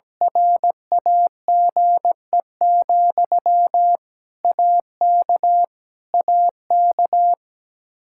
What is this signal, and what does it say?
Morse code, heard as a keyed tone, about 17 wpm, 700 Hz. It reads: RAGE, AK AK
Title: Practice QSO - Exchange 1 - Beginning - 17wpm